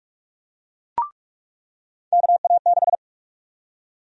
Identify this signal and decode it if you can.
Morse code, heard as a keyed tone, about 45 wpm, 700 Hz. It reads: KA6
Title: USA Call Sign Prefixes - 45wpm